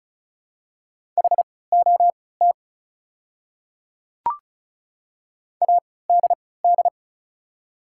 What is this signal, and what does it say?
Morse code, heard as a keyed tone, about 36 wpm, spaced as for 20 wpm, 700 Hz. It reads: HOT ADD
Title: 3x Character Spacing - Top 300 Words - 35wpm